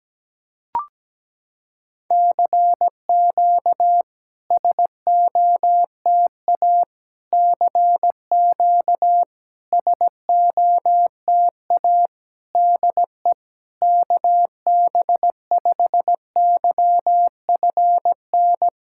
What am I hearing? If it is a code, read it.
Morse code, heard as a keyed tone, about 17 wpm, 700 Hz. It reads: CQ SOTA CQ SOTA DE KB5YFN
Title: SOTA QSO Practice 17wpm